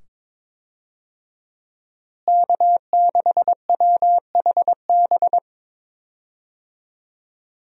Morse code, heard as a keyed tone, about 22 wpm, 700 Hz. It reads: K6WHB
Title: USA Call Signs - 5 Characters - Repeated 2x - 22wpm